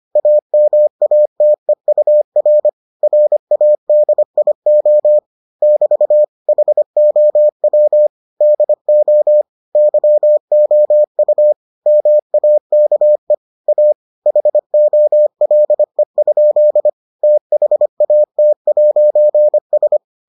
Morse code, heard as a keyed tone, about 25 wpm, 600 Hz. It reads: AMATEUR RADIO <BT> HOW DO YOU MAKE A HOLE? THAT'S